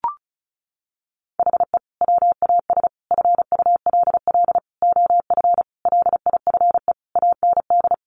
Morse code, heard as a keyed tone, about 34 wpm, 700 Hz. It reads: HE WAS FULL OF LIFE AND
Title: Sentences from Top 1100 Words 35wpm